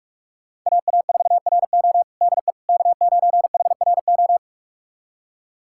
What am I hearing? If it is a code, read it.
Morse code, heard as a keyed tone, about 46 wpm, 700 Hz. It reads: AA4RO DE K9HRO
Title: USA Callsign DE Callsign - 5 Characters - 45wpm